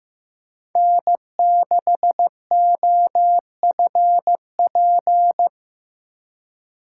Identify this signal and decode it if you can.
Morse code, heard as a keyed tone, about 15 wpm, 700 Hz. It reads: N6OFP
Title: USA Call Signs - 5 Characters - 8x Character Spacing - 15wpm